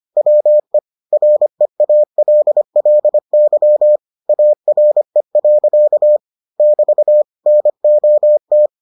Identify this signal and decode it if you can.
Morse code, heard as a keyed tone, about 25 wpm, 600 Hz. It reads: WE REALLY ARE. <BT> NOT